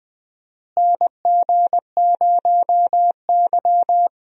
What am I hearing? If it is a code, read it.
Morse code, heard as a keyed tone, about 20 wpm, 700 Hz. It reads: NG0Y
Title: USA Call Signs - 4 Characters - Repeated 2x - 20wpm